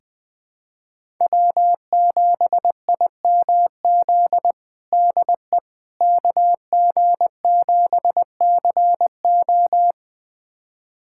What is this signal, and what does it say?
Morse code, heard as a keyed tone, about 21 wpm, 700 Hz. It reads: W7IMZ DE KG7CO
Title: USA Callsign DE Callsign - 5 Characters - 20wpm